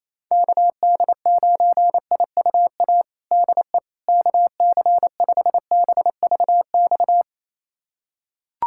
Morse code, heard as a keyed tone, about 28 wpm, 700 Hz. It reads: KD9IUA DE KC5BVX
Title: USA Callsign DE Callsign 28wpm